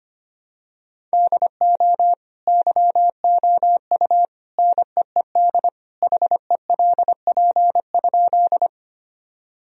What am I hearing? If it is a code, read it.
Morse code, heard as a keyed tone, about 25 wpm, 700 Hz. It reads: DO YOU NEED HELP?
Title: Sentences from Top 1000 Words - Review Part 1 - 25wpm